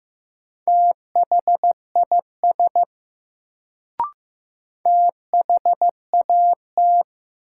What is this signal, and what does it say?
Morse code, heard as a keyed tone, about 16 wpm, 700 Hz. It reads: THIS THAT
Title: Sets of 2 Words - New Word Drill - 15wpm